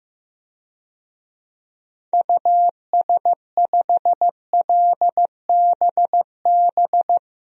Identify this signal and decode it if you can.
Morse code, heard as a keyed tone, about 16 wpm, 700 Hz. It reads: US5LBB
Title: DXCC-100 Call Signs - 6 Characters - Repeated 2x - 15wpm